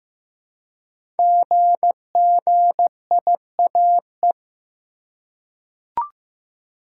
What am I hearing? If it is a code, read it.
Morse code, heard as a keyed tone, about 15 wpm, 700 Hz. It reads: GGIAE
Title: Code Groups - 5 Letters - 15wpm